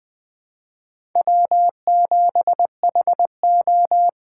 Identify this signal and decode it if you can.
Morse code, heard as a keyed tone, about 20 wpm, 700 Hz. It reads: W7HO
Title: 2023 SOTA Chaser Call Signs - 20wpm